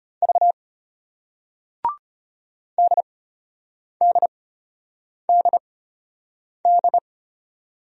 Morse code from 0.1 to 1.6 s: V

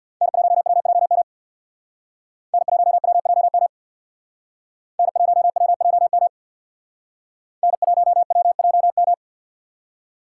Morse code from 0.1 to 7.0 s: N1WJM N1WJM N1WJM

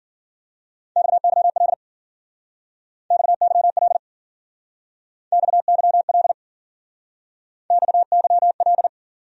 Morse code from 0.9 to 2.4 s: XYL

